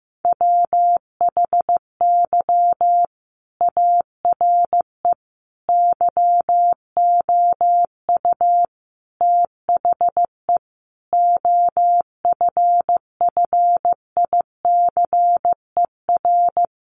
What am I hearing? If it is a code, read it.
Morse code, heard as a keyed tone, about 15 wpm, 700 Hz. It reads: WHY ARE YOU THE OFFICER
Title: Sentences from Top 1200 Words 15wpm